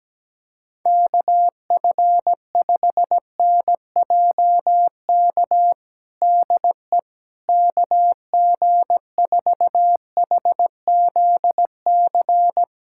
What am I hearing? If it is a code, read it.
Morse code, heard as a keyed tone, about 17 wpm, 700 Hz. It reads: KF5NJK DE KG4HZC
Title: USA Callsign DE Callsign - 6 Characters - 17wpm